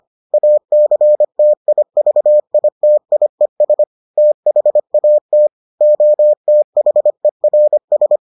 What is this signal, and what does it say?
Morse code, heard as a keyed tone, about 25 wpm, 600 Hz. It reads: ACTIVITIES THAT OTHERS